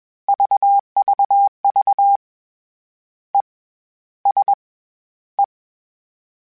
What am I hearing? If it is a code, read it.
Morse code, heard as a keyed tone, about 21 wpm, 800 Hz. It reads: VVV E S E
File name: Short Runs / Short Runs-mp3s / ESE @20